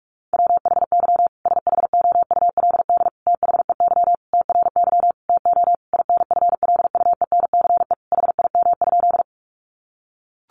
Morse code, heard as a keyed tone, about 44 wpm, 700 Hz. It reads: WHY SHOULD THEY TRY TO INFLUENCE HIM?
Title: Sentences from Top 1100 Words 45wpm